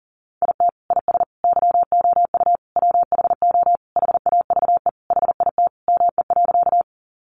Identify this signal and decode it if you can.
Morse code, heard as a keyed tone, about 37 wpm, 700 Hz. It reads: IT IS YOU WHO HAVE HIT ME.
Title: Sentences from Top 1200 Words 40wpm